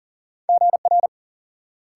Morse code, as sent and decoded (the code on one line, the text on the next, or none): --. .-.
GR